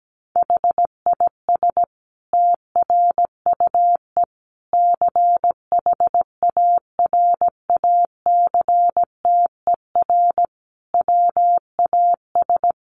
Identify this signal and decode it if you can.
Morse code, heard as a keyed tone, about 17 wpm, 700 Hz. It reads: HIS TRUE CHARACTER WAS
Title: Sentences from Top 1200 Words 17wpm